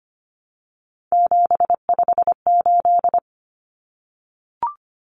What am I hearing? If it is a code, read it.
Morse code, heard as a keyed tone, about 25 wpm, 700 Hz. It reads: 758
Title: Common Cut Numbers - 3-Digit 25wpm